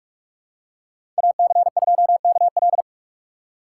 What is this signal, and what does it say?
Morse code, heard as a keyed tone, about 46 wpm, 700 Hz. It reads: AC2KL
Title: USA Call Signs - 2x Character Spacing - 45wpm